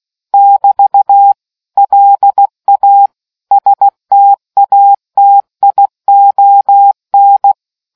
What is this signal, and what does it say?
Morse code, heard as a keyed tone, about 16 wpm, 800 Hz. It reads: <BT> LA STATION